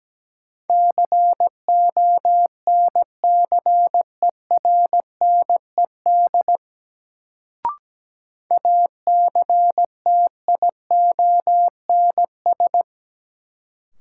Morse code, heard as a keyed tone, about 17 wpm, 700 Hz. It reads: CONCERNED ACTIONS
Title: Sentences from Top 1000 Words - Review Part 6 - New Word Drill - 17wpm